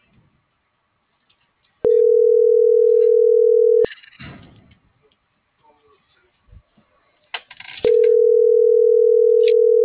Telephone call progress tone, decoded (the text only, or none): ringback tone